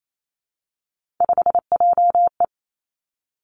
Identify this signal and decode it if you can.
Morse code, heard as a keyed tone, about 28 wpm, 700 Hz. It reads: 5JE
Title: Three Challenging Letter-Number Combinations 28wpm